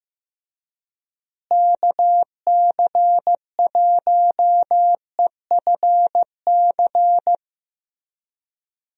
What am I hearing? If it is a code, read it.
Morse code, heard as a keyed tone, about 15 wpm, 700 Hz. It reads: KC1EFC
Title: USA Call Signs - 6 Characters - Repeated 3x - 15wpm